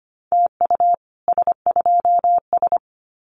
Morse code, heard as a keyed tone, about 25 wpm, 700 Hz. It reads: TU S2S
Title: Common QSO Phrases Encore 25wpm